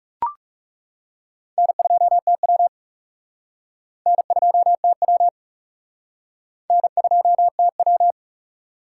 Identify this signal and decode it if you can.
Morse code, heard as a keyed tone, about 37 wpm, 700 Hz. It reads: N2TW N2TW N2TW
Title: Speed Racing - USA 1x2 Call Signs 30wpm